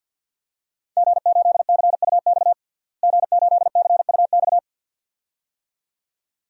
Morse code, heard as a keyed tone, about 51 wpm, 700 Hz. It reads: G8CUX G8CUX
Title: DXCC-100 Call Signs - 5 Characters - Repeated 2x - 50wpm